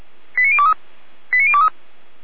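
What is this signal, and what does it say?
Five-tone selective call (ZVEI), heard as quick runs of five tones, tones of about 70 ms. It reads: 89023 89023
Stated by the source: ZVEI five-tone selective call received by SDR, code 89305, archived by Feuerwehr Orlamünde.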